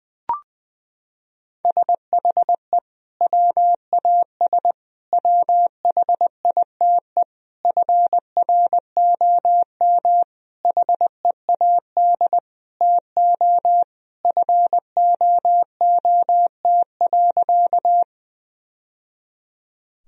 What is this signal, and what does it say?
Morse code, heard as a keyed tone, about 20 wpm, 700 Hz. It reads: SHE WAS WHITE FROM HEAD TO FOOT.
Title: Sentences from Top 1000 Words - Review Part 3 - 20wpm